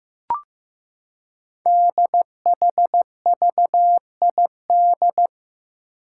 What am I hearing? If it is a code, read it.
Morse code, heard as a keyed tone, about 15 wpm, 700 Hz. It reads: DHVID
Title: Code Groups - 5 Letters - 15wpm